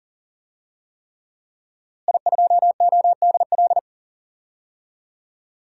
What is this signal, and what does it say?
Morse code, heard as a keyed tone, about 37 wpm, 700 Hz. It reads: I2ODL
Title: DXCC-100 Call Signs - 5 Characters - Repeated 3x - 40wpm